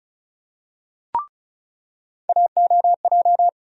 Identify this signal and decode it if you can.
Morse code, heard as a keyed tone, about 34 wpm, 700 Hz. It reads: AOJ
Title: USA Call Sign Postfixes - 3 Letters 35wpm